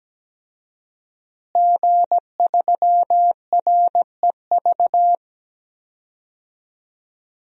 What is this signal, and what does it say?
Morse code, heard as a keyed tone, about 17 wpm, 700 Hz. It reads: G3REV
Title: DXCC-100 Call Signs - 5 Characters - Repeated 3x - 17wpm